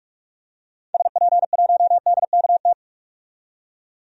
Morse code, heard as a keyed tone, about 46 wpm, 700 Hz. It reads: SP1DKT
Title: DXCC-100 Call Signs - 6 Characters - 45wpm